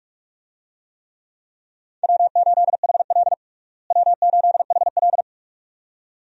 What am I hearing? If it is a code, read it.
Morse code, heard as a keyed tone, about 42 wpm, 700 Hz. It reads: W8HL W8HL
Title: USA Call Signs - 4 Characters - Repeated 2x - 45wpm